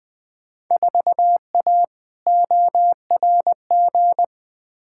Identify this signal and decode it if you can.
Morse code, heard as a keyed tone, about 20 wpm, 700 Hz. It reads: 4A ORG
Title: Field Day Exchanges - 20wpm